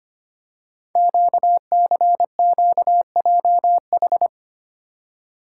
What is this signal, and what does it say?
Morse code, heard as a keyed tone, about 25 wpm, 700 Hz. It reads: QCQJH
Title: Code Groups - 5 Letters - 25wpm